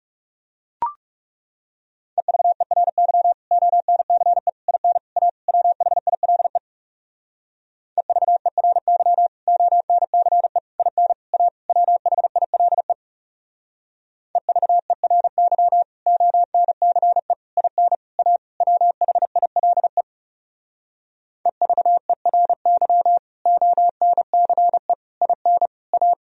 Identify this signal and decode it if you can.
Morse code, heard as a keyed tone, about 38 wpm, 700 Hz. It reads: EVERY ONCE IN A WHILE EVERY ONCE IN A WHILE EVERY ONCE IN A WHILE EVERY ONCE IN A
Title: Speed-Racing - Sets of 5 Words 30wpm